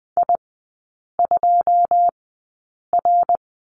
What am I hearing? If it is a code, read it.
Morse code, heard as a keyed tone, about 19 wpm, spaced as for 8 wpm, 700 Hz. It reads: I2R
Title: 166 Lesson 27e - Number 7 - Three Characters 20wpm